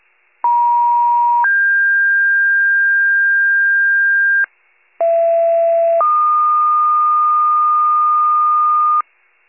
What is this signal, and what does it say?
Two-tone page, heard as a one-second tone, then a three-second tone, about 950 then 1600 Hz, about 670 then 1220 Hz, with noise. It is heard from 0.4 to 9.0 s.
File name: wcalert